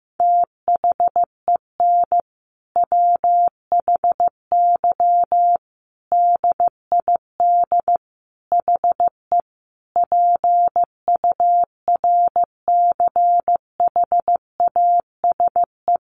Morse code, heard as a keyed tone, about 15 wpm, 700 Hz. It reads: THEN WHY DID HE PURCHASE
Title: Sentences from Top 800 Words 15wpm